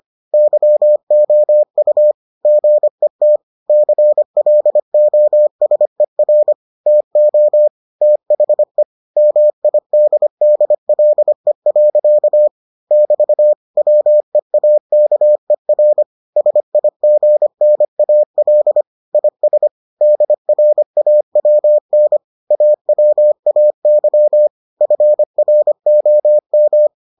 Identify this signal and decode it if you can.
Morse code, heard as a keyed tone, about 25 wpm, 600 Hz. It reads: YOU GET CLOSER TO THE MIDDLE. <BT> WEAKER SIGNAL IS DRAWN AWAY FROM